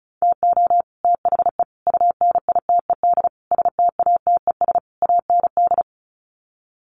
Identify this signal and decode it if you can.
Morse code, heard as a keyed tone, about 34 wpm, 700 Hz. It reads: TO THE UNITED STATES AND
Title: Sets of 5 Words Encore 35wpm